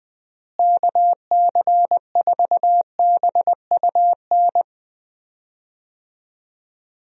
Morse code, heard as a keyed tone, about 20 wpm, 700 Hz. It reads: KC4BUN
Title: USA Call Signs - 6 Characters - Repeated 2x - 20wpm